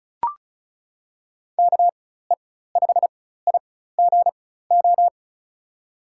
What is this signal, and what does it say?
Morse code, heard as a keyed tone, about 36 wpm, spaced as for 16 wpm, 700 Hz. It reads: KE5IGO
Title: USA Call Signs - 4x Character Spacing - 35wpm